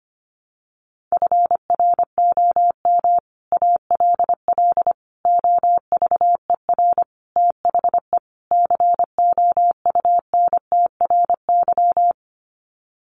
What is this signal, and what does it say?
Morse code, heard as a keyed tone, about 25 wpm, 700 Hz. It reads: FROM ALL OVER THE COUNTRY
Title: Sets of 5 Words Encore 25wpm